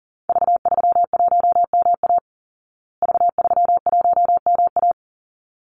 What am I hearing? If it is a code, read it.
Morse code, heard as a keyed tone, about 40 wpm, 700 Hz. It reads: V31MA V31MA
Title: Call Signs Repeated 3x - 40wpm